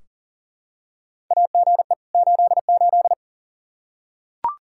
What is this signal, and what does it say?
Morse code, heard as a keyed tone, about 40 wpm, 700 Hz. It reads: AGE 88
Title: Practice QSO - Exchange 3 - More Info - 40wpm